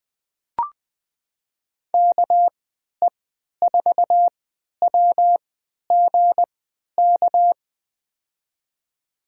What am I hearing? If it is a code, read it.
Morse code, heard as a keyed tone, about 20 wpm, spaced as for 11 wpm, 700 Hz. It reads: KE4WGK
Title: USA Call Signs - 6 Characters - 3x Character Spacing - 20wpm